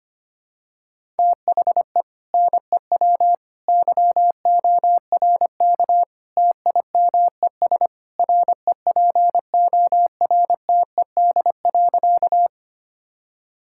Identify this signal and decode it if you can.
Morse code, heard as a keyed tone, about 25 wpm, 700 Hz. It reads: THE NEW YORK TIMES REPORTED.
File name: practice / 25wpm / Sentences with 3-Word Phrases - 25wpm